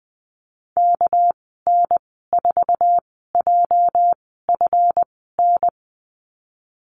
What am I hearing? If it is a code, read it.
Morse code, heard as a keyed tone, about 20 wpm, spaced as for 14 wpm, 700 Hz. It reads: KN4JFN
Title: 231 Lesson 37g - Letter x - Call Signs 20wpm